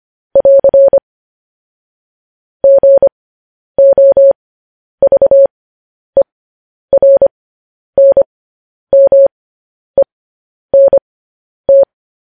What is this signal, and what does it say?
Morse code, heard as a keyed tone, about 25 wpm, spaced as for 10 wpm, 550 Hz. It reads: <AR> GOVERNMENT
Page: news-headlines-in-morse-code-at-10-wpm